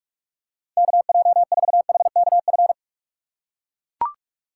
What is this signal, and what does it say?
Morse code, heard as a keyed tone, about 44 wpm, 700 Hz. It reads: KJ4HKF